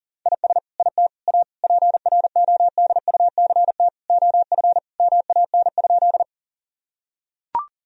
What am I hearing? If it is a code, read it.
Morse code, heard as a keyed tone, about 37 wpm, 700 Hz. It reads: IS IT A PRODUCT OF MAN?
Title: Sentences from Top 1000 Words - Review Part 2 - 40wpm